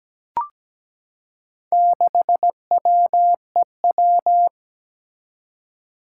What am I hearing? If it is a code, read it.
Morse code, heard as a keyed tone, about 17 wpm, 700 Hz. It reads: 6WEW
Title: Four Challenging Letter-Number Combinations 17wpm